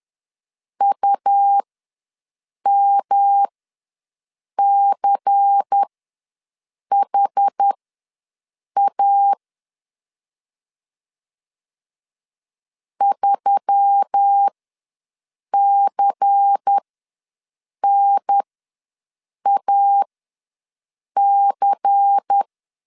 Morse code, heard as a keyed tone, about 10 wpm, spaced as for 6 wpm, 800 Hz. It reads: UMCHA 3CNAC
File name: LEZIONI / 5 LBVC / 5 LBVC ws 10wpm 3733